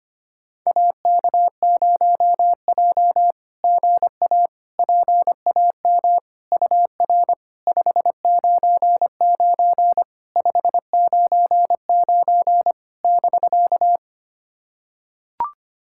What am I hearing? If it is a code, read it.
Morse code, heard as a keyed tone, about 25 wpm, 700 Hz. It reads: AK0J GA PAM UR 599 599 <BK>
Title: SOTA QSO Practice 25wpm